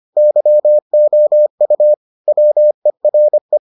Morse code, heard as a keyed tone, about 25 wpm, 600 Hz. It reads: YOU WERE